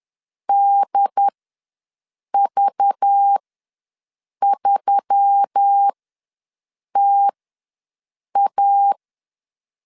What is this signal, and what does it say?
Morse code, heard as a keyed tone, about 10 wpm, spaced as for 6 wpm, 800 Hz. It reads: DV3TA
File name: LEZIONI / 5 LBVC / 5 LBV ws 10wpm 3645